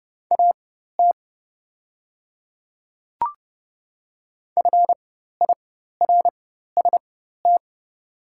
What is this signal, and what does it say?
Morse code, heard as a keyed tone, about 30 wpm, spaced as for 14 wpm, 700 Hz. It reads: AT FIRST
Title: 4x Character Spacing - Top 300 Words - 30wpm